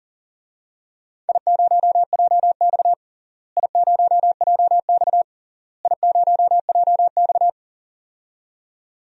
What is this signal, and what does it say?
Morse code, heard as a keyed tone, about 40 wpm, 700 Hz. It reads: I0JX I0JX I0JX
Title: DXCC-100 Call Signs - 4 Characters - Repeated 3x - 40wpm